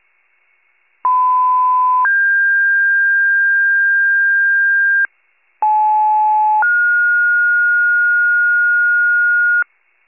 Two-tone page, heard as a one-second tone, then a three-second tone, about 1010 then 1600 Hz, about 860 then 1430 Hz, with noise. It is heard from 1.0 to 9.6 s.
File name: pager / 75 FIRE / wcalert